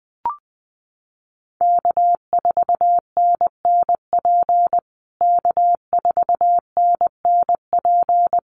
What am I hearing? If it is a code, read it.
Morse code, heard as a keyed tone, about 20 wpm, 700 Hz. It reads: K4NNP K4NNP
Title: USA Call Signs Repeated 3x 20wpm